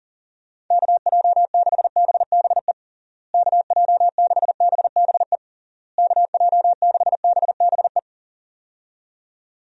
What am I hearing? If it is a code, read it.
Morse code, heard as a keyed tone, about 40 wpm, 700 Hz. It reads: KJ6BBE KJ6BBE KJ6BBE
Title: USA Call Signs - 6 Characters - Repeated 3x - 40wpm